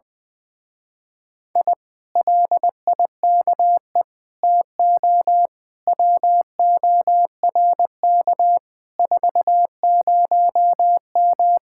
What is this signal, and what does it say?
Morse code, heard as a keyed tone, about 20 wpm, 700 Hz. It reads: I LIKE TO WORK 40M
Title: Speed-Racing Common QSO Phrases - 20wpm